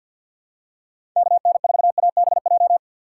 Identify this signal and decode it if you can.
Morse code, heard as a keyed tone, about 47 wpm, 700 Hz. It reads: KN4ABJ
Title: USA Call Signs - 6 Characters - 6x Character Spacing - 50wpm